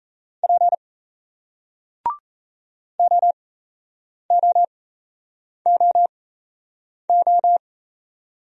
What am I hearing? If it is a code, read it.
Morse code, heard as a keyed tone, about 32 wpm, spaced as for 8 wpm, 700 Hz. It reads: P OOOO
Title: Speed-Racing Single Letter-Number 28wpm